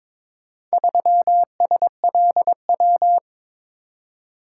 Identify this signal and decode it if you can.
Morse code, heard as a keyed tone, about 22 wpm, 700 Hz. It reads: 3SLW
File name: practice / 22wpm / Four Letter-Number Combinations - 22wpm